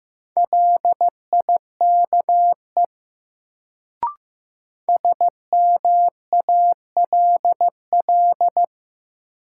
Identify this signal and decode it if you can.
Morse code, heard as a keyed tone, about 15 wpm, 700 Hz. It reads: LIKE SMALL
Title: Sets of 2 Words - New Word Drill - 15wpm